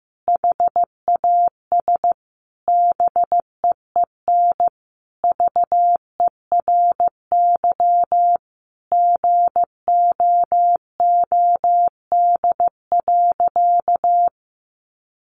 Morse code, heard as a keyed tone, about 15 wpm, 700 Hz. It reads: HAS BEEN VERY GOOD.
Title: Sentences from Top 1400 Words 15wpm